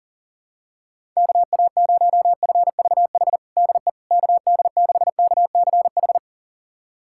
Morse code, heard as a keyed tone, about 40 wpm, 700 Hz. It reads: KA0FVH DE KD6KCH
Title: USA Callsign DE Callsign - 6 Characters - 40wpm